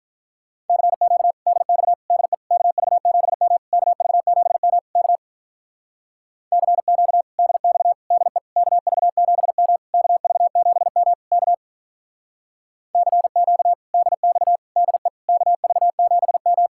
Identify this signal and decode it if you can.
Morse code, heard as a keyed tone, about 45 wpm, 700 Hz. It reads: CQ DX DE KV7M KV7M K CQ DX DE KV7M KV7M K CQ DX DE KV7M